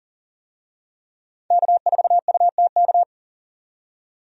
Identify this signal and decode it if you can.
Morse code, heard as a keyed tone, about 40 wpm, 700 Hz. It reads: K4UTK
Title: USA Call Signs - 5 Characters - 40wpm